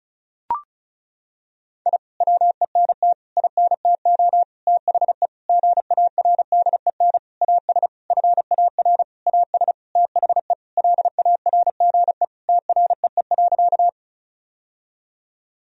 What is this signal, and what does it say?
Morse code, heard as a keyed tone, about 34 wpm, 700 Hz. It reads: I WENT INTO THE GARDEN AS FAR AS THE LARGE TREE.
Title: Sentences from Top 1000 Words - Review Part 4 - 35wpm